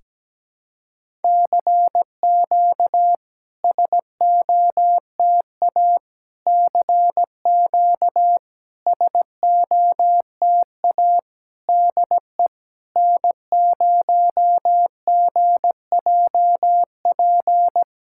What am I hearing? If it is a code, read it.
Morse code, heard as a keyed tone, about 17 wpm, 690 Hz. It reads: CQ SOTA CQ SOTA DE N0GJP
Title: SOTA QSO Practice 17wpm